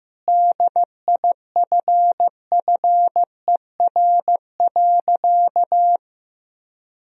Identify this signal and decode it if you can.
Morse code, heard as a keyed tone, about 14 wpm, 700 Hz. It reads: DIFFER.
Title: Sentences from Top 1000 Words - Review Part 1 - 15wpm